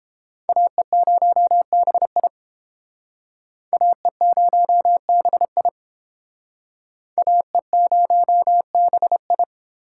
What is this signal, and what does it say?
Morse code, heard as a keyed tone, about 30 wpm, 700 Hz. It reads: AE0BI AE0BI AE0BI